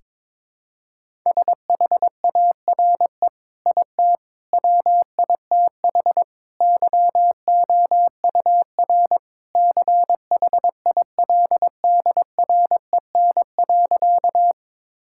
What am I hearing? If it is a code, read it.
Morse code, heard as a keyed tone, about 22 wpm, 700 Hz. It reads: SHARE IT WITH YOUR CHILDREN.
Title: Sentences from Top 1000 Words - Review Part 6 - 22wpm